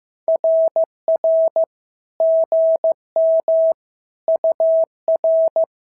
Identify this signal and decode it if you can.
Morse code, heard as a keyed tone, about 15 wpm, 650 Hz. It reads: RR GM UR
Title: SOTA QSO Practice 15wpm